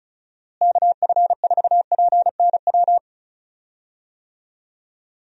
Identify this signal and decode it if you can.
Morse code, heard as a keyed tone, about 36 wpm, 700 Hz. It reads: KF4PNW
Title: USA Call Signs - 6 Characters - Repeated 2x - 35wpm